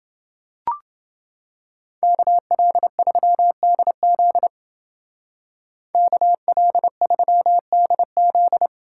Morse code, heard as a keyed tone, about 28 wpm, 700 Hz. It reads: KL3DZ KL3DZ